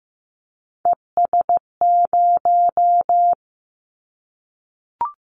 Morse code, heard as a keyed tone, about 15 wpm, 700 Hz. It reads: ES0
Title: Callsign Prefixes for Top 50 DXCC - 15wpm